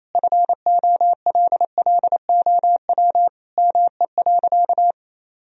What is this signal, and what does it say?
Morse code, heard as a keyed tone, about 27 wpm, 700 Hz. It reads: FOLLOW ME.
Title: Sentences from Top 1000 Words - Review Part 2 - 28wpm